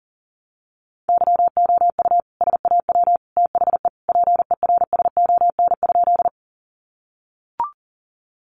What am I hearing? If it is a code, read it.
Morse code, heard as a keyed tone, about 40 wpm, 700 Hz. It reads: YOU SAW THE PERSON?
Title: Sentences from Top 400 Words 40wpm